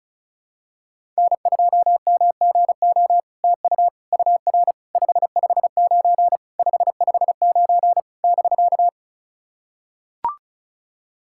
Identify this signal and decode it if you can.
Morse code, heard as a keyed tone, about 36 wpm, 700 Hz. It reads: N2MGO TU UR 559 559 <BK>
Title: SOTA QSO Practice 35wpm